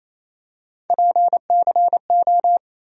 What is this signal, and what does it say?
Morse code, heard as a keyed tone, about 28 wpm, 700 Hz. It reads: PCO
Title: USA Call Sign Postfixes - 3 Letters 28wpm